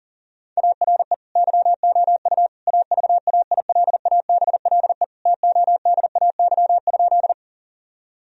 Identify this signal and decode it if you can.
Morse code, heard as a keyed tone, about 40 wpm, 700 Hz. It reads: ARE YOU AVAILABLE TODAY?
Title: Sentences from Top 1000 Words - Review Part 3 - 40wpm